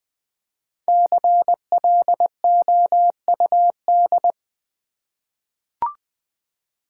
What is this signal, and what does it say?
Morse code, heard as a keyed tone, about 20 wpm, 700 Hz. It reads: CLOUD
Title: Easy Words with Hints - 20wpm